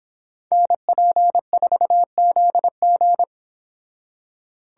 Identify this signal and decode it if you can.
Morse code, heard as a keyed tone, about 26 wpm, 700 Hz. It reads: NP4ZG